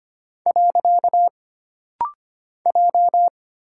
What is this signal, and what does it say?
Morse code, heard as a keyed tone, about 25 wpm, 700 Hz. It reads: . J
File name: Single Character (Letter:Number:Punctuation:Prosign) - Rapid-Fire - 25wpm